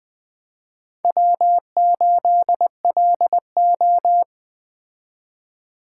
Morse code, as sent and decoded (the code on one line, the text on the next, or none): .-- ---.. .-.. ---
W8LO